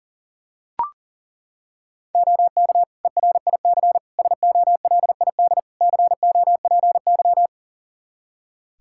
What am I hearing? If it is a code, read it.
Morse code, heard as a keyed tone, about 40 wpm, 700 Hz. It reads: OK ERIC SOLID COPY
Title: Practice QSO - Exchange 3 - More Info - 40wpm